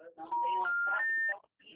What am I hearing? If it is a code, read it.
Special information tone, with speech.